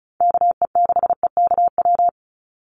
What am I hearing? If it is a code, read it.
Morse code, heard as a keyed tone, about 34 wpm, 700 Hz. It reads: KE6EKW
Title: USA Call Signs 35wpm